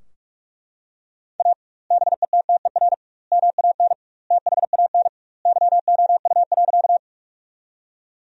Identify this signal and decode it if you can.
Morse code, heard as a keyed tone, about 42 wpm, 700 Hz. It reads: A BETTER MAN THAN YOU.
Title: 2x Word-Spacing - Sentences from Top 400 Words - 45wpm